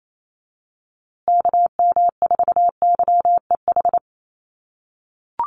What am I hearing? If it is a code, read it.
Morse code, heard as a keyed tone, about 28 wpm, 700 Hz. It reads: KM4YEH
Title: USA Call Signs Repeated 2x 28wpm